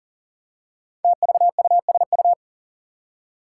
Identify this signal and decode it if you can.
Morse code, heard as a keyed tone, about 42 wpm, 700 Hz. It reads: TVUSU
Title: Five Challenging Letter-Number Combinations 40wpm